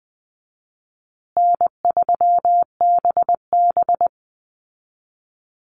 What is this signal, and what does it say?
Morse code, heard as a keyed tone, about 20 wpm, 700 Hz. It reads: N3BB
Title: Easy Call Signs Repeated 3x - 20wpm